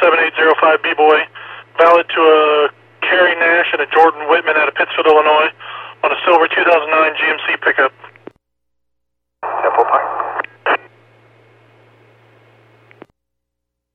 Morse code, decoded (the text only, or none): none